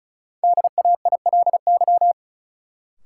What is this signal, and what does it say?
Morse code, heard as a keyed tone, about 36 wpm, 700 Hz. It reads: DAILY
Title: Sentences from Top 1100 Words - New Word Drill Alt - 35wpm